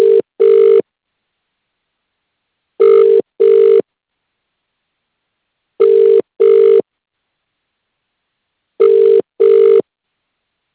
Ringback tone (UK), 400 + 450 Hz.